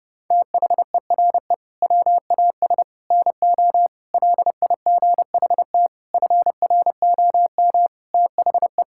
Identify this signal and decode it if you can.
Morse code, heard as a keyed tone, about 30 wpm, 700 Hz. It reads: THERE WAS NO LIGHT FROM THE
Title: Sentences from Top 1000 Words - Review Part 5 - 30wpm